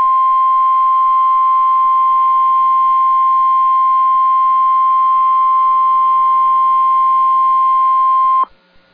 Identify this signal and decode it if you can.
Emergency Alert System (SAME) tones (the 1050 Hz attention tone). It is heard from 0.3 to 7.9 s.